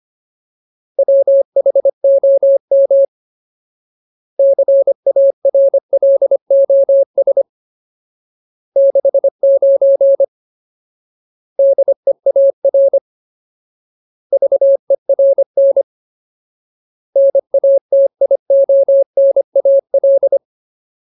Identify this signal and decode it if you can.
Morse code, heard as a keyed tone, about 25 wpm, 550 Hz. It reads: WHOM CARLOS 69 DEAR VERN NATIONAL
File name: Ultimate Word List / 25WPM Ultimate